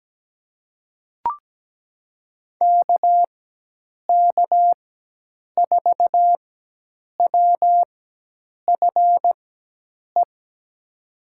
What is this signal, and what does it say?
Morse code, heard as a keyed tone, about 17 wpm, spaced as for 8 wpm, 700 Hz. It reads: KK4WFE
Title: USA Call Signs - 6 Characters - 4x Character Spacing - 17wpm